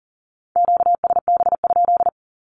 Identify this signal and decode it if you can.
Morse code, heard as a keyed tone, about 41 wpm, 700 Hz. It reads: QSB?
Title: Q-Codes with Spoken Meaning - 40wpm